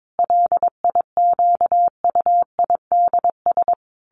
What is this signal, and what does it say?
Morse code, heard as a keyed tone, about 22 wpm, 700 Hz. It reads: LIQUIDS
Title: A Sound Of Thunder 22wpm